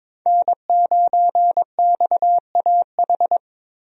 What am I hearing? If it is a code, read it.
Morse code, heard as a keyed tone, about 22 wpm, 700 Hz. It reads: N9XAH
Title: USA Call Signs - 5 Characters - Repeated 2x - 22wpm